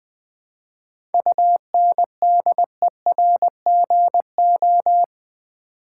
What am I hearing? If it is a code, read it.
Morse code, heard as a keyed tone, about 20 wpm, 700 Hz. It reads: UNDERGO
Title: 7 Letter Words - First 3 Letters Given - 20wpm